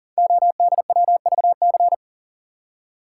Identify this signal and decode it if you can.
Morse code, heard as a keyed tone, about 38 wpm, 700 Hz. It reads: ODWVC
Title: Code Groups - 5 Letters - 40wpm